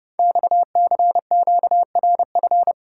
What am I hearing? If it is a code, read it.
Morse code, heard as a keyed tone, about 30 wpm, 690 Hz. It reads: XCQRF
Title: Code Groups - 5 Letters - 30wpm